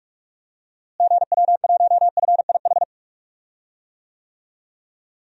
Morse code, heard as a keyed tone, about 46 wpm, 700 Hz. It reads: GW1FIH
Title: DXCC-100 Call Signs - 6 Characters - Repeated 3x - 45wpm